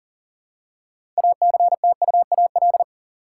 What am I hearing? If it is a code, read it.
Morse code, heard as a keyed tone, about 40 wpm, 700 Hz. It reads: ACTUAL
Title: Six Letter Words - 40wpm